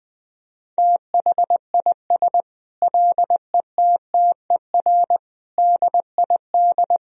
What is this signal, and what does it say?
Morse code, heard as a keyed tone, about 20 wpm, 700 Hz. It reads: THIS LETTER DID
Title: Sentences from Top 1000 Words - Review Part 4 - 20wpm